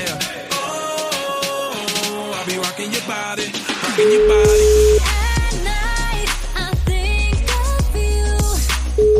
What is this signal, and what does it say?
Ringback tone (Europe, 425 Hz), with noise.